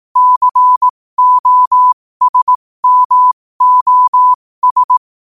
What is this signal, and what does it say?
Morse code, heard as a keyed tone, about 18 wpm, spaced as for 15 wpm, 1000 Hz. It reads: COSMOS
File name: qotdfiles / 15wpm